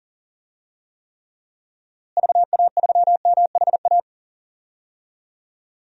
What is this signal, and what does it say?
Morse code, heard as a keyed tone, about 40 wpm, 700 Hz. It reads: VA3MHA
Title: DXCC-100 Call Signs - 6 Characters - Repeated 2x - 40wpm